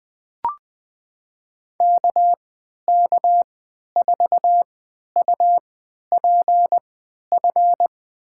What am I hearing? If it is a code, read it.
Morse code, heard as a keyed tone, about 20 wpm, spaced as for 11 wpm, 700 Hz. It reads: KK4UPF
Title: USA Call Signs - 6 Characters - 3x Character Spacing - 20wpm